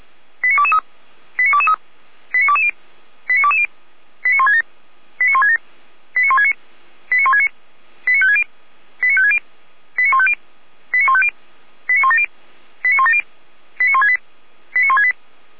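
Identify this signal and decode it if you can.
Five-tone selective call (ZVEI), heard as quick runs of five tones, tones of about 70 ms. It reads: 89223 89223 89220 89220 89167 89167 89169 89169 89560 89560 89150 89150 89170 89170 89168 89168
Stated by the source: ZVEI five-tone selective call received on SDR.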